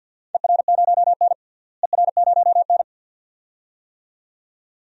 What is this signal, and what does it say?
Morse code, heard as a keyed tone, about 46 wpm, 700 Hz. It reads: ER0N ER0N
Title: DXCC-100 Call Signs - 4 Characters - Repeated 2x - 50wpm